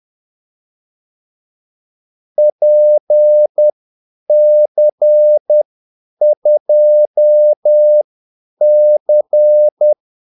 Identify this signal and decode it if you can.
Morse code, heard as a keyed tone, about 10 wpm, spaced as for 8 wpm, 600 Hz. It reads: PC2C
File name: Callsigns-10-8wpm